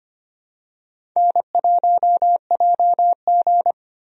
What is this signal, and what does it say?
Morse code, heard as a keyed tone, about 25 wpm, 700 Hz. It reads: N1JG
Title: USA Call Signs - 4 Characters - Repeated 3x - 25wpm